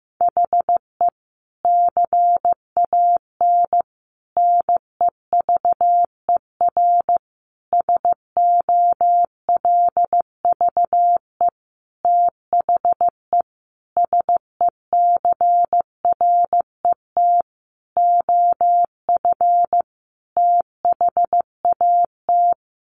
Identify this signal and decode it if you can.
Morse code, heard as a keyed tone, about 15 wpm, 700 Hz. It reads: HE CAN NEVER SOLVE THE SECRET OF THAT
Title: Sentences from Top 1300 Words 15wpm